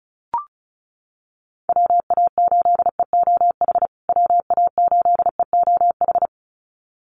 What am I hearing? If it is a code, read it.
Morse code, heard as a keyed tone, about 34 wpm, 700 Hz. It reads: WA8EOH WA8EOH
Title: USA Call Signs Repeated 2x 35wpm